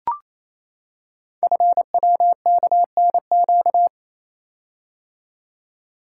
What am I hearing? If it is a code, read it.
Morse code, heard as a keyed tone, about 27 wpm, 690 Hz. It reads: FWKNQ
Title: Code Groups - 5 Letters - 28wpm